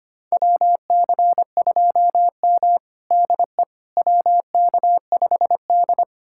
Morse code, heard as a keyed tone, about 25 wpm, 700 Hz. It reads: WC2M DE WK5D
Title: USA Callsign DE Callsign - 4 Characters - 25wpm